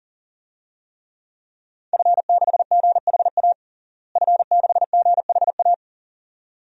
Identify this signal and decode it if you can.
Morse code, heard as a keyed tone, about 40 wpm, 700 Hz. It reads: F6GHA F6GHA
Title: DXCC-100 Call Signs - 5 Characters - Repeated 2x - 40wpm